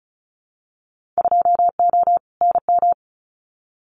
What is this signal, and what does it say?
Morse code, heard as a keyed tone, about 34 wpm, 700 Hz. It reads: 2O NM
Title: Winter Field Day Exchanges 35wpm